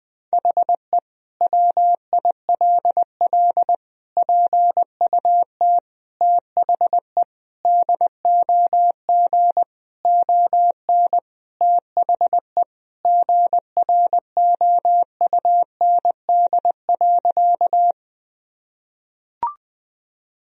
Morse code, heard as a keyed tone, about 21 wpm, 700 Hz. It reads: HE WILL PUT THE DOG ON THE GROUND.
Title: Sentences from Top 1000 Words - Review Part 2 - 20wpm